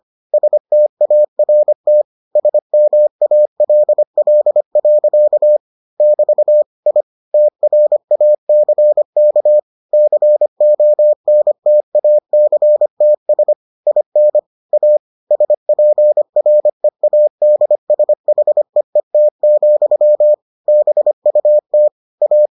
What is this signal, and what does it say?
Morse code, heard as a keyed tone, about 25 wpm, 600 Hz. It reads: START SMALL. <BT> I TRACK CONTACTS IN A SPREADSHEET, BUT A